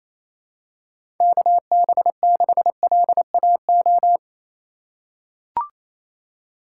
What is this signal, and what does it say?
Morse code, heard as a keyed tone, about 28 wpm, 700 Hz. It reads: KB6LAO
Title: USA Call Signs - 6 Characters - 3x Character Spacing - 28wpm